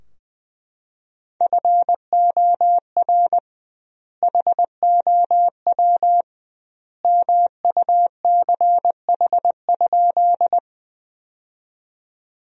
Morse code, heard as a keyed tone, about 20 wpm, 700 Hz. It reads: FOR HOW MUCH?
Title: 2x Word-Spacing - Sentences from Top 200 Words - 20wpm